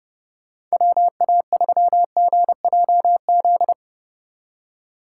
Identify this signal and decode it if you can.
Morse code, heard as a keyed tone, about 28 wpm, 700 Hz. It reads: WA3GJZ
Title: Speed Racing - USA Call Signs 30wpm